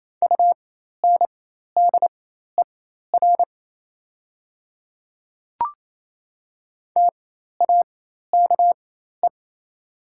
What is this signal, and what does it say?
Morse code, heard as a keyed tone, about 28 wpm, spaced as for 13 wpm, 700 Hz. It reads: UNDER TAKE
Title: 4x Character Spacing - Top 200 Words - 28wpm